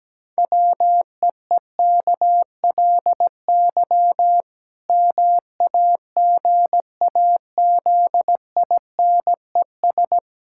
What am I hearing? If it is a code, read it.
Morse code, heard as a keyed tone, about 17 wpm, 700 Hz. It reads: WEEKLY MAGAZINES
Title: Pangram Challenge v2 - 17wpm